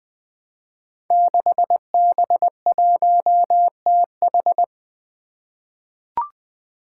Morse code, heard as a keyed tone, about 20 wpm, 700 Hz. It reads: 6B1TH
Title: Five Challenging Letter-Number Combinations 20wpm